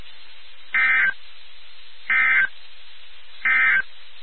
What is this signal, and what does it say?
Emergency Alert System (SAME) tones, heard as harsh buzzing bursts. It is heard from 0.6 to 3.9 s.
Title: NOAA Weather Radio WXK79